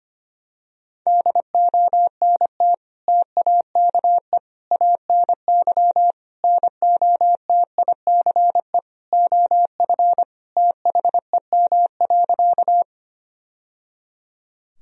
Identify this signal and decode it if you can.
Morse code, heard as a keyed tone, about 25 wpm, 700 Hz. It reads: DONT TAKE ANY NOTICE OF THEM.
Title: Sentences from Top 1000 Words - Review Part 2 - 25wpm